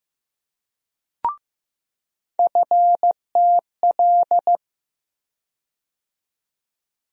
Morse code, heard as a keyed tone, about 15 wpm, 700 Hz. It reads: FTL
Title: USA Call Sign Postfixes - 3 Letters 15wpm